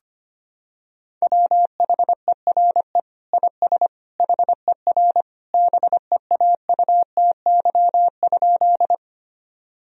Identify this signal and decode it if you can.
Morse code, heard as a keyed tone, about 25 wpm, 700 Hz. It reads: WHERE IS HER BEAUTY?
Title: Sentences from Top 1000 Words - Review Part 2 - 25wpm